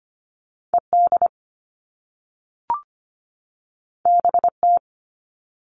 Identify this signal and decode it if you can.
Morse code, heard as a keyed tone, about 25 wpm, 700 Hz. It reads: E D B T
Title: All Cut Numbers - 2-Digit 25wpm